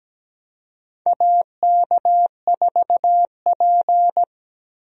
Morse code, heard as a keyed tone, about 17 wpm, 700 Hz. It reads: AK4P
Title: USA Call Signs - 4 Characters - Repeated 2x - 17wpm